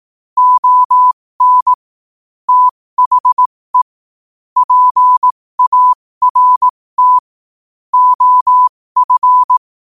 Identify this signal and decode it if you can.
Morse code, heard as a keyed tone, about 18 wpm, spaced as for 15 wpm, 1000 Hz. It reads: ON THE PART OF
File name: qotdfiles / 15wpm